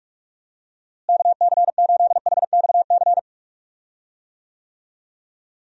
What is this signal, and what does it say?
Morse code, heard as a keyed tone, about 45 wpm, 700 Hz. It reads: KC8HXC
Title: USA Call Signs - 6 Characters - Repeated 2x - 45wpm